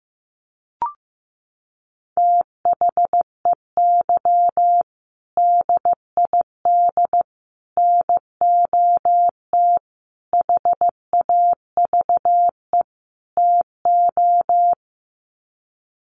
Morse code, heard as a keyed tone, about 14 wpm, 700 Hz. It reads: THEY DID NOT HAVE TO
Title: Sets of 5 Words Encore2 15wpm